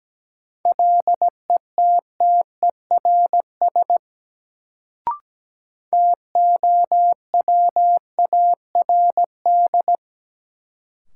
Morse code, heard as a keyed tone, about 17 wpm, 700 Hz. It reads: LETTERS TOWARD
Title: Sentences from Top 1000 Words - Review Part 6 - New Word Drill - 17wpm